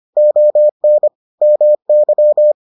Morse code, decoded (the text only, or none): ON MY